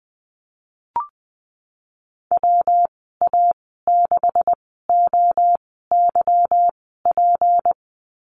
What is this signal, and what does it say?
Morse code, heard as a keyed tone, about 20 wpm, spaced as for 15 wpm, 700 Hz. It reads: WA6OYP
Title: 238 Lesson 38g - Letter q - Call Signs 20wpm